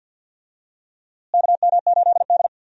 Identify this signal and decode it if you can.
Morse code, heard as a keyed tone, about 52 wpm, 700 Hz. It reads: KM8D
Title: USA Call Signs - 4 Characters - Repeated 2x - 50wpm